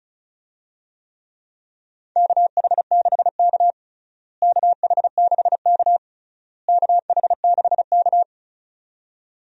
Morse code, heard as a keyed tone, about 34 wpm, 700 Hz. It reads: KH6K KH6K KH6K
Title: DXCC-100 Call Signs - 4 Characters - Repeated 3x - 35wpm